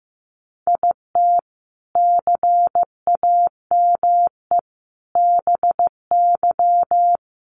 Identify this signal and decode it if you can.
Morse code, heard as a keyed tone, about 15 wpm, 700 Hz. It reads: IT CAME BY
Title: Sentences from Top 1000 Words 15wpm